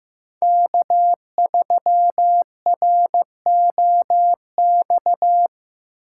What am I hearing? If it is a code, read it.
Morse code, heard as a keyed tone, about 15 wpm, 700 Hz. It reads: K3ROX
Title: USA Call Signs - 3x Character Spacing - 15wpm